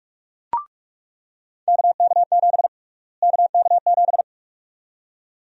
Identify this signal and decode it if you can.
Morse code, heard as a keyed tone, about 46 wpm, 700 Hz. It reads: KK7 KK7
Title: USA Call Sign Prefixes - Repeated 2x - 45wpm